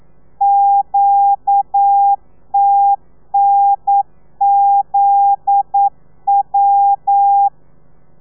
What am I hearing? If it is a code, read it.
Morse code, heard as a keyed tone, about 9 wpm, 800 Hz. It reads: QTNZW
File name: telegrafia / i2tfj / Lez9 40